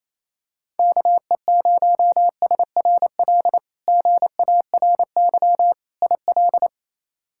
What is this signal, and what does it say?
Morse code, heard as a keyed tone, about 28 wpm, 700 Hz. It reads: KE0SRL GARY IL
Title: K1USN SST QSO Practice - 28wpm